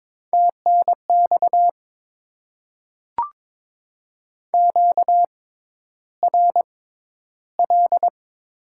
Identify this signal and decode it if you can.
Morse code, heard as a keyed tone, about 22 wpm, spaced as for 11 wpm, 700 Hz. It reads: TNX QRL
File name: Top 20 QSO Elements - 6x Character Spacing - 22wpm